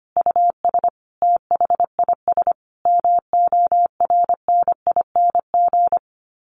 Morse code, heard as a keyed tone, about 25 wpm, 700 Hz. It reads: US THIS MORNING
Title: Sets of 5 Words Encore 25wpm